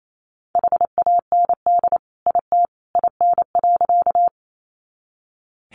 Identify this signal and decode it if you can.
Morse code, heard as a keyed tone, about 28 wpm, 700 Hz. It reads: HAND IT IN.
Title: Sentences from Top 200 Words 28wpm